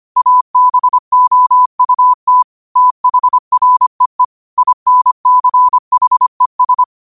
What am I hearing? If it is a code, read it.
Morse code, heard as a keyed tone, about 25 wpm, 1000 Hz. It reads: ABOUT THREE INCHES
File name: 25wpm